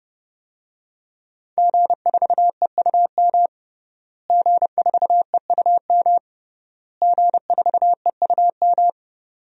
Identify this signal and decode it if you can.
Morse code, heard as a keyed tone, about 30 wpm, 700 Hz. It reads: G4EUM G4EUM G4EUM
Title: DXCC-100 Call Signs - 5 Characters - Repeated 3x - 30wpm